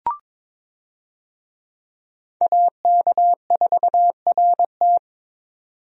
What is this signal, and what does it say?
Morse code, heard as a keyed tone, about 22 wpm, 700 Hz. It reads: AK4RT
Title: USA Call Signs - 5 Characters - 22wpm